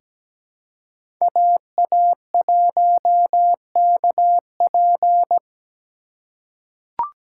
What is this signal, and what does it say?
Morse code, heard as a keyed tone, about 17 wpm, 700 Hz. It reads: AA1KP
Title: USA Call Signs - 5 Characters - 3x Character Spacing - 17wpm